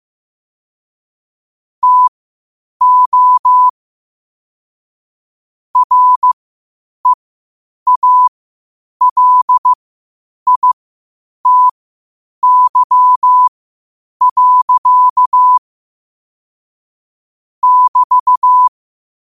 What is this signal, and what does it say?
Morse code, heard as a keyed tone, about 15 wpm, spaced as for 8 wpm, 1000 Hz. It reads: TO REALITY. <BT>
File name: qotdfiles / 8wpm